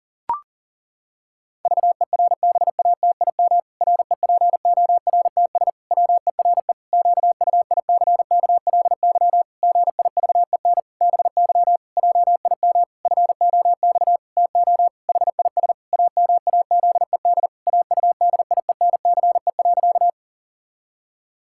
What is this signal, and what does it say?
Morse code, heard as a keyed tone, about 37 wpm, 700 Hz. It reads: VERBATIM REPORTS WERE QUICKLY GIVEN BY JIM FOX TO HIS AMAZED AUDIENCE.
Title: Pangram Challenge v2 - 40wpm